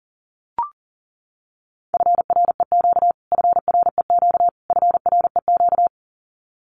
Morse code, heard as a keyed tone, about 37 wpm, 700 Hz. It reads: FREQ FREQ FREQ
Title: Abbreviations Repeated 3x - 40wpm